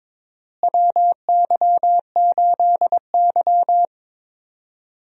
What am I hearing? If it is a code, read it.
Morse code, heard as a keyed tone, about 22 wpm, 700 Hz. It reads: WY8Y
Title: USA Call Signs - 4 Characters - Repeated 2x - 22wpm